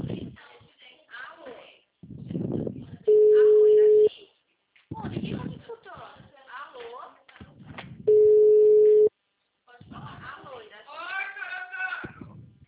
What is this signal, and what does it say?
Ringback tone (Europe, 425 Hz), with speech.